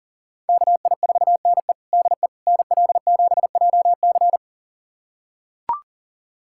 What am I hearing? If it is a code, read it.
Morse code, heard as a keyed tone, about 40 wpm, 700 Hz. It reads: KI4NE DE NL7JC
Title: USA Callsign DE Callsign 40wpm